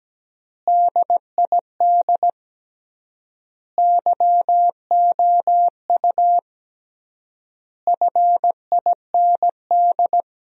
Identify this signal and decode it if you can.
Morse code, heard as a keyed tone, about 17 wpm, 700 Hz. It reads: DID YOU FIND
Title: 3x Word-Spacing - Sentences from Top 200 Words - 17wpm